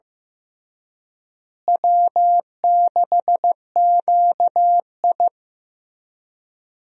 Morse code, heard as a keyed tone, about 15 wpm, 700 Hz. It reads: W6QI
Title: USA Call Signs - 4 Characters - Repeated 3x - 15wpm